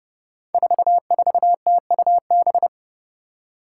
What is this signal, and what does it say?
Morse code, heard as a keyed tone, about 28 wpm, 700 Hz. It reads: 44TUB
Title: Five Challenging Letter-Number Combinations 30wpm